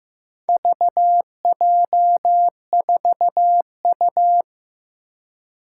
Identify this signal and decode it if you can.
Morse code, heard as a keyed tone, about 15 wpm, 700 Hz. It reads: VJ4U
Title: Four Challenging Letter-Number Combinations 15wpm